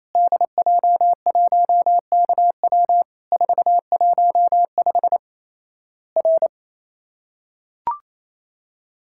Morse code, heard as a keyed tone, about 28 wpm, 700 Hz. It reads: DJ1KW 415 R
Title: Worked All Europe DX Contest - QTC Exchange - 28wpm